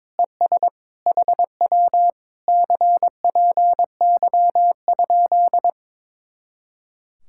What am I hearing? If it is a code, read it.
Morse code, heard as a keyed tone, about 22 wpm, 700 Hz. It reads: ES HW CPY?
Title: Speed-Racing Common QSO Phrases - 22wpm